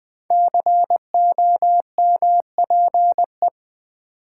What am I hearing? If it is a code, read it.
Morse code, heard as a keyed tone, about 20 wpm, 700 Hz. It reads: COMPE
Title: Build A Word - 5 Letters At A Time - 10-Letter Words - 20wpm